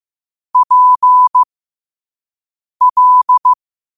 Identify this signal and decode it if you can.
Morse code, heard as a keyed tone, about 15 wpm, spaced as for 5 wpm, 1000 Hz. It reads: PL